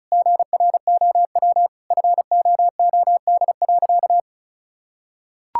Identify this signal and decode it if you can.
Morse code, heard as a keyed tone, about 34 wpm, 700 Hz. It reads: GROW FOOD.
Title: Sentences from Top 1000 Words - Review Part 6 - 35wpm